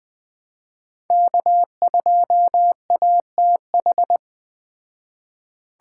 Morse code, heard as a keyed tone, about 20 wpm, 700 Hz. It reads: K2ATH
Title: Speed Racing - USA 1x3 Call Signs 17wpm